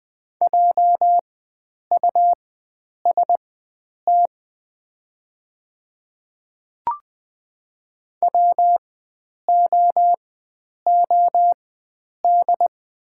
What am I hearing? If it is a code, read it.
Morse code, heard as a keyed tone, about 20 wpm, spaced as for 9 wpm, 700 Hz. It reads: JUST WOOD
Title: 4x Character Spacing - Top 300 Words - 20wpm